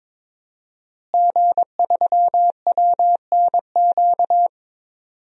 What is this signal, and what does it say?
Morse code, heard as a keyed tone, about 22 wpm, 700 Hz. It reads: G3WNQ
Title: DXCC-100 Call Signs - 5 Characters - Repeated 3x - 22wpm